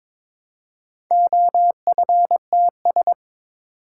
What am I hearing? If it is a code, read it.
Morse code, heard as a keyed tone, about 22 wpm, 700 Hz. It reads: OFTS
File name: practice / 22wpm / Four Letter-Number Combinations - 22wpm